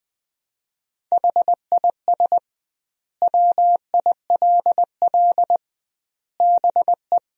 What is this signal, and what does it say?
Morse code, heard as a keyed tone, about 21 wpm, 700 Hz. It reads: HIS WILL BE
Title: 2x Word-Spacing - Sentences from Top 500 Words - 20wpm